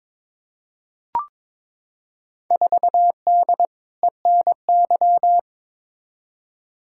Morse code, heard as a keyed tone, about 22 wpm, 700 Hz. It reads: 4D ENY
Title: Field Day Exchanges Encore - 22wpm